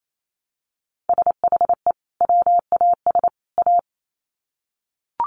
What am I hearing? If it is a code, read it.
Morse code, heard as a keyed tone, about 28 wpm, 700 Hz. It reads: SHE WAS A
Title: Sets of 3 Words 28wpm